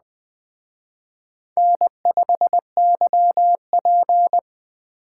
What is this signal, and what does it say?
Morse code, heard as a keyed tone, about 20 wpm, 700 Hz. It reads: N5YP